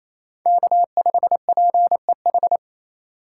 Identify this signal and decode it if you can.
Morse code, heard as a keyed tone, about 28 wpm, 700 Hz. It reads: K5PEH
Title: USA Call Signs - 5 Characters - 4x Character Spacing - 28wpm